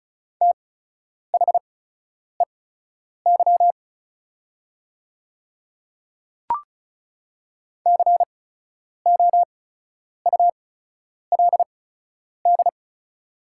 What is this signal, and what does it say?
Morse code, heard as a keyed tone, about 36 wpm, spaced as for 10 wpm, 690 Hz. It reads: THEY COULD
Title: 8x Character Spacing - Top 100 Words - 35wpm